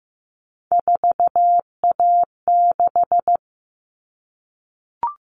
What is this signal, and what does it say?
Morse code, heard as a keyed tone, about 15 wpm, 700 Hz. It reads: 4A6
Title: All Cut Numbers - 3-Digit 15wpm